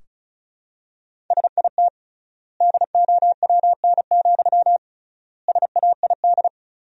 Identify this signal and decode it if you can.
Morse code, heard as a keyed tone, about 34 wpm, 700 Hz. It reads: SIT DOWN, SAID
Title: 3x Word-Spacing - Sentences from Top 500 Words - 35wpm